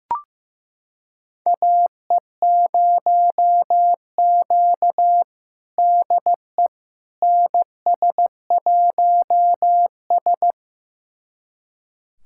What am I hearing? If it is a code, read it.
Morse code, heard as a keyed tone, about 15 wpm, 700 Hz. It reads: AE0Q DE NS1S
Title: USA Callsign DE Callsign - 4 Characters - 15wpm